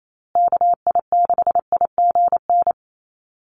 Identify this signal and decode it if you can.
Morse code, heard as a keyed tone, about 28 wpm, 700 Hz. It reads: KI6IGN
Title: USA Call Signs Repeated 2x 28wpm